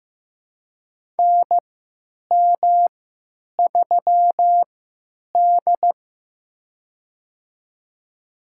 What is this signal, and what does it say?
Morse code, heard as a keyed tone, about 15 wpm, spaced as for 9 wpm, 700 Hz. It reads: NM3D
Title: USA Call Signs - 4 Characters - 3x Character Spacing - 15wpm